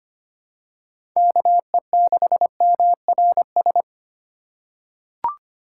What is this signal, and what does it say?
Morse code, heard as a keyed tone, about 25 wpm, 700 Hz. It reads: KE6MRS
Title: USA Call Signs - 6 Characters - 3x Character Spacing - 25wpm